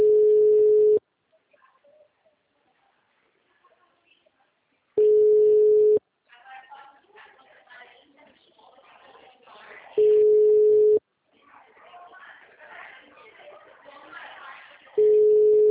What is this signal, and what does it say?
Ringback tone (Europe, 425 Hz), with speech.